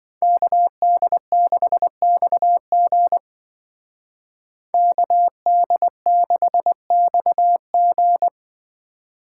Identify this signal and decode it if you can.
Morse code, heard as a keyed tone, about 26 wpm, 700 Hz. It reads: KD6XG KD6XG